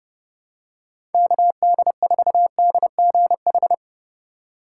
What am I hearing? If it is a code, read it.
Morse code, heard as a keyed tone, about 32 wpm, 700 Hz. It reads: KD4DGH
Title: USA Call Signs - 8x Character Spacing - 30wpm